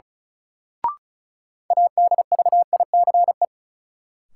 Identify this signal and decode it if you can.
Morse code, heard as a keyed tone, about 34 wpm, 700 Hz. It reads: ADVICE
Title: Sentences from Top 1100 Words - New Word Drill Alt - 35wpm